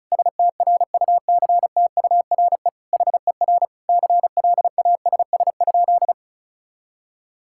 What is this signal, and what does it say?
Morse code, heard as a keyed tone, about 36 wpm, 700 Hz. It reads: STRUCTURE HER CLASS?
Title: Sentences from Top 1000 Words - Review Part 6 - 35wpm